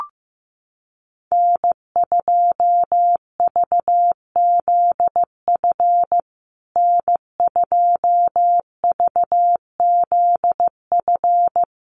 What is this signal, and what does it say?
Morse code, heard as a keyed tone, about 15 wpm, 700 Hz. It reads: N2VZF N2VZF
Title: USA Call Signs Repeated 3x 15wpm